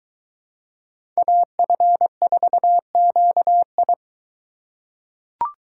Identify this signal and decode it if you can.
Morse code, heard as a keyed tone, about 22 wpm, 700 Hz. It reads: AF4QI